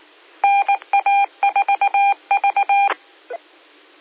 Morse code, heard as a keyed tone, about 19 wpm, 800 Hz. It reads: NA4V
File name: qsorec SimplexLogic 17-28-17